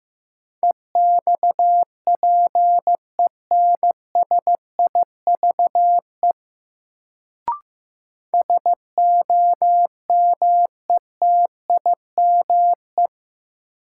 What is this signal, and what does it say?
Morse code, heard as a keyed tone, about 15 wpm, 700 Hz. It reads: EXPENSIVE SOMETIME
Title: Sentences from Top 1500 Words - New Word Drill Alt - 15wpm